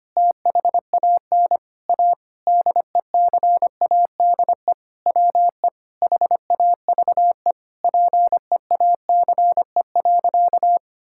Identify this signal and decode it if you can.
Morse code, heard as a keyed tone, about 25 wpm, 700 Hz. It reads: THAN A DECADE WE HAVE PEACE.
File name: practice / 25wpm / Sentences with 9-Word Phrases - 25wpm